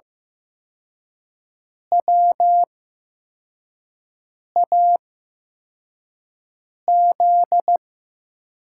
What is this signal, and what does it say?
Morse code, heard as a keyed tone, about 15 wpm, spaced as for 4 wpm, 700 Hz. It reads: WAZ
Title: USA Call Signs - 8x Character Spacing - 15wpm